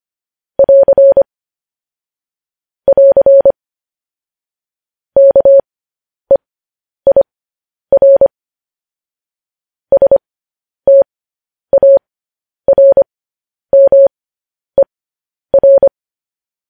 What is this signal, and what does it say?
Morse code, heard as a keyed tone, about 25 wpm, spaced as for 10 wpm, 550 Hz. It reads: <AR> <AR> KEIR STARMER